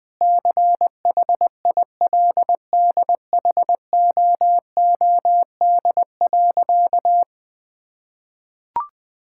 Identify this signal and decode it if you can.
Morse code, heard as a keyed tone, about 21 wpm, 700 Hz. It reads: CHILDHOOD.
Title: Sentences with 2-Word Phrases - 20wpm